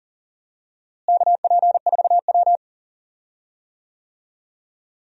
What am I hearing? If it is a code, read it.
Morse code, heard as a keyed tone, about 40 wpm, 700 Hz. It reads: KP4W
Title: DXCC-100 Call Signs - 4 Characters - Repeated 3x - 40wpm